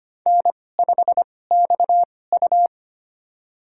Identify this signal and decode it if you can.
Morse code, heard as a keyed tone, about 25 wpm, spaced as for 18 wpm, 700 Hz. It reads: N5XU
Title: USA Call Signs - 4 Characters - 2x Character Spacing - 25wpm